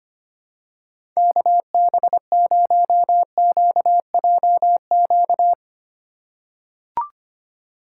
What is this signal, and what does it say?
Morse code, heard as a keyed tone, about 25 wpm, 700 Hz. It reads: KB0QJQ
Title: USA Call Signs - 6 Characters - 6x Character Spacing - 25wpm